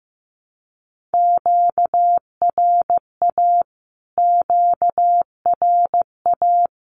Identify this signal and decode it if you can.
Morse code, heard as a keyed tone, about 15 wpm, 700 Hz. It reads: QRA QRA
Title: Q-Codes Repeated 3x 15wpm